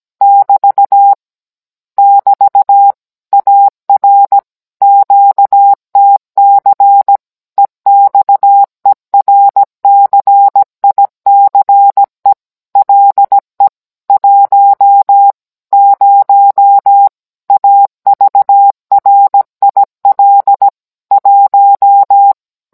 Morse code, heard as a keyed tone, about 17 wpm, 800 Hz. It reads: <BT> <BT> AR QTC EXERCICE LE 10 AVRIL 1